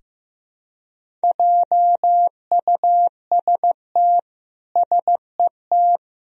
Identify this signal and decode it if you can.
Morse code, heard as a keyed tone, about 16 wpm, 700 Hz. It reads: JUST SET
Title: Sentences from Top 300 Words and Abbreviations 15wpm